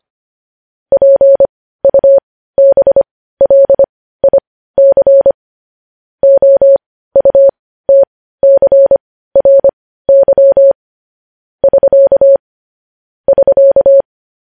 Morse code, heard as a keyed tone, about 25 wpm, spaced as for 15 wpm, 550 Hz. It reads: PUBLIC OUTCRY <SK> <SK>